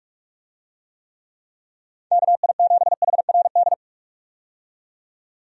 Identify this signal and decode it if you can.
Morse code, heard as a keyed tone, about 46 wpm, 700 Hz. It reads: KI7HRD
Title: USA Call Signs - 6 Characters - 45wpm